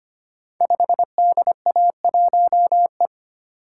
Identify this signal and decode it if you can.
Morse code, heard as a keyed tone, about 25 wpm, 700 Hz. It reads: 5DA1E
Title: Code Groups - 5 Letter:Numbers - 25wpm